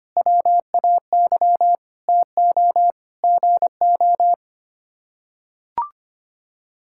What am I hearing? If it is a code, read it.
Morse code, heard as a keyed tone, about 25 wpm, 700 Hz. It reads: WAY TO GO
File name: Sets of 9 Words - 25wpm